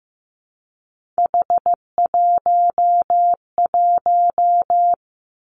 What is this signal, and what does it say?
Morse code, heard as a keyed tone, about 15 wpm, 700 Hz. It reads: H11
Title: Three Challenging Letter-Number Combinations 15wpm